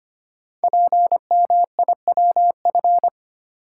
Morse code, heard as a keyed tone, about 25 wpm, 700 Hz. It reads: PMIWF
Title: Code Groups - 5 Letter:Numbers - 25wpm